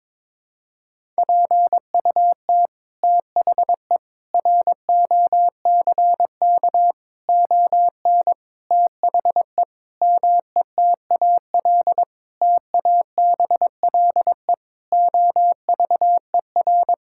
Morse code, heard as a keyed tone, about 22 wpm, 700 Hz. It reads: PUT THE ROCK ON THE METAL TABLE OVER
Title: Sentences from Top 1000 Words - Review Part 6 - 22wpm